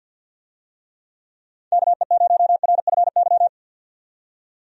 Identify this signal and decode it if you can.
Morse code, heard as a keyed tone, about 51 wpm, 700 Hz. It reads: KE0RFY
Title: USA Call Signs - 6 Characters - 50wpm